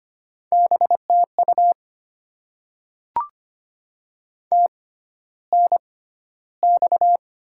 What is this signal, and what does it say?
Morse code, heard as a keyed tone, about 25 wpm, spaced as for 13 wpm, 700 Hz. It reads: BTU TNX
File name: Top 20 QSO Elements - 6x Character Spacing - 25wpm